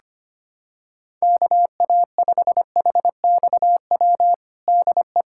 Morse code, heard as a keyed tone, about 25 wpm, 700 Hz. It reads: KA5HXW DE
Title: USA Callsign DE Callsign 25wpm